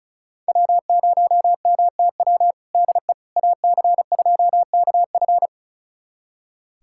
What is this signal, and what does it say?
Morse code, heard as a keyed tone, about 34 wpm, 700 Hz. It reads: W0MTW DE AC2KF
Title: USA Callsign DE Callsign - 5 Characters - 35wpm